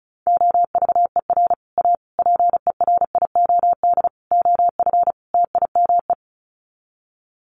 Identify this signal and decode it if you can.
Morse code, heard as a keyed tone, about 34 wpm, 700 Hz. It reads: OVER A PERIOD OF TIME
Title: Sets of 5 Words Encore2 35wpm